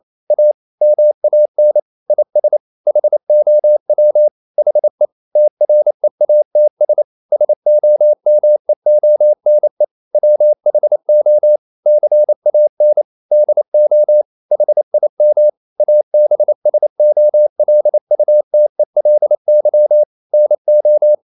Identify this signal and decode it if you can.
Morse code, heard as a keyed tone, about 28 wpm, 600 Hz. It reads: A MAN IS HOW HE TREATS SOMEONE WHO CAN DO HIM ABSOLUTELY NO
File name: CW / Practice / EnglishQuotations0228